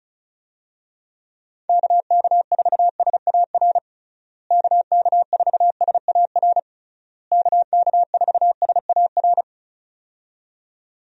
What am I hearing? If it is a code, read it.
Morse code, heard as a keyed tone, about 34 wpm, 700 Hz. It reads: KK4SAR KK4SAR KK4SAR
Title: USA Call Signs - 6 Characters - Repeated 3x - 35wpm